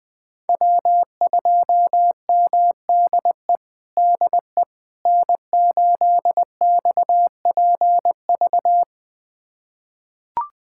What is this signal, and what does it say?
Morse code, heard as a keyed tone, about 21 wpm, 700 Hz. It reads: W2MDE DE N8XPV
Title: USA Callsign DE Callsign - 5 Characters - 20wpm